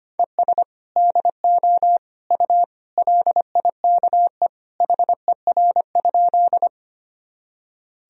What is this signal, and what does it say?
Morse code, heard as a keyed tone, about 25 wpm, 700 Hz. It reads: ES DO U LIKE HER?
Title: Sentences from Top 100 Words and Abbreviations 25wpm